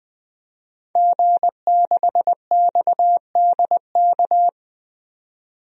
Morse code, heard as a keyed tone, about 20 wpm, 700 Hz. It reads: G6XDK
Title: DXCC-100 Call Signs - 5 Characters - 20wpm